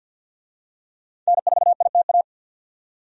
Morse code, heard as a keyed tone, about 51 wpm, 700 Hz. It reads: N4ITA
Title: USA Call Signs - 5 Characters - 50wpm